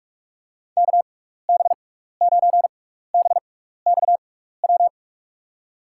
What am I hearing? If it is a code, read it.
Morse code, heard as a keyed tone, about 41 wpm, spaced as for 15 wpm, 700 Hz. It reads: KB9BXW
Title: USA Call Signs - 6 Characters - 6x Character Spacing - 45wpm